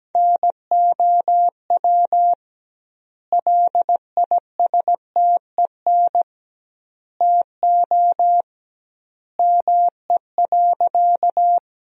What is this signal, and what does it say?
Morse code, heard as a keyed tone, about 17 wpm, 700 Hz. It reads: NOW LISTEN TO ME.
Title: 2x Word-Spacing - Sentences from Top 400 Words - 17wpm